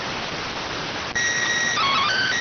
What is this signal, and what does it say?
SSTV (slow-scan television), PD 290.